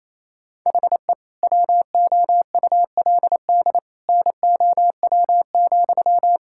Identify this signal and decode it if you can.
Morse code, heard as a keyed tone, about 28 wpm, 700 Hz. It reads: HE WOULD NOW,
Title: Sentences from Top 1000 Words - Review Part 5 - 28wpm